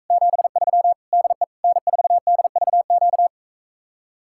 Morse code, heard as a keyed tone, about 45 wpm, 700 Hz. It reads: 73 DE N4DVQ